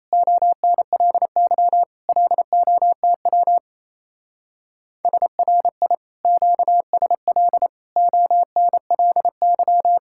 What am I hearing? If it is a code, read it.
Morse code, heard as a keyed tone, about 37 wpm, 700 Hz. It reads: ONLY LOTW SRI QSL ONLY